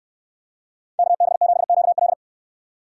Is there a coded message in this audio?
Yes: Morse code from 1.0 to 2.1 s, heard as a keyed tone, about 64 wpm, 700 Hz. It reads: KB7QL